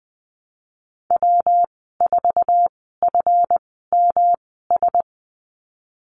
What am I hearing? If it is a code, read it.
Morse code, heard as a keyed tone, about 19 wpm, spaced as for 14 wpm, 700 Hz. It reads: W4FMS
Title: 140 Lesson 23g - Letter f - Call Signs 20wpm